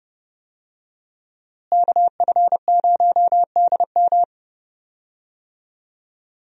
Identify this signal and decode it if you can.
Morse code, heard as a keyed tone, about 29 wpm, 700 Hz. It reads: KF0DM
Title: USA Call Signs - 5 Characters - Repeated 2x - 30wpm